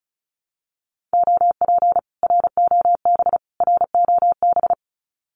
Morse code, heard as a keyed tone, about 34 wpm, 700 Hz. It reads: OP ROB ROB
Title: Common QSO Phrases Encore 35wpm